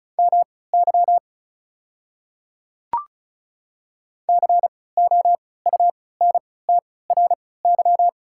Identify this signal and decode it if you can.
Morse code, heard as a keyed tone, about 36 wpm, spaced as for 20 wpm, 700 Hz. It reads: MY COUNTRY
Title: 3x Character Spacing - Top 500 Words - 35wpm